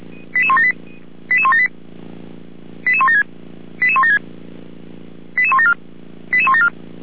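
Five-tone selective call (ZVEI), heard as quick runs of five tones, tones of about 70 ms. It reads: 88178 88178 88176 88176 88174 88174
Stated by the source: ZVEI five-tone call 88171 received on SDR.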